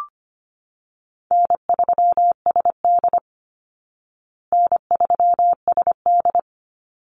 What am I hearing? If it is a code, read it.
Morse code, heard as a keyed tone, about 25 wpm, 700 Hz. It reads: N3SD N3SD
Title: Easy Call Signs Repeated 3x - 25wpm